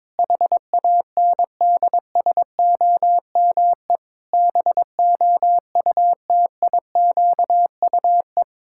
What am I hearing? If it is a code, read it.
Morse code, heard as a keyed tone, about 22 wpm, 700 Hz. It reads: HANDSOME BOUTIQUE
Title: Pangram Challenge v2 - 22wpm